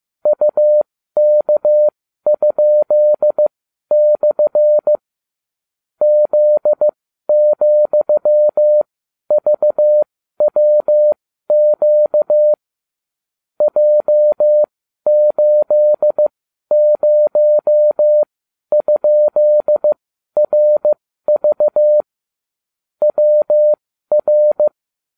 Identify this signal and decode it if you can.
Morse code, heard as a keyed tone, about 15 wpm, spaced as for 12 wpm, 600 Hz. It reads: UK?/ Z,VWQ J80?RV WR